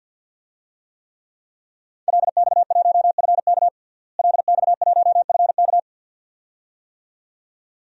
Morse code, heard as a keyed tone, about 52 wpm, 700 Hz. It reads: LX1FK LX1FK
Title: DXCC-100 Call Signs - 5 Characters - Repeated 2x - 50wpm